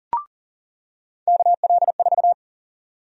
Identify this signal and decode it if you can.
Morse code, heard as a keyed tone, about 38 wpm, 700 Hz. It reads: KL4
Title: USA Call Sign Prefixes - 40wpm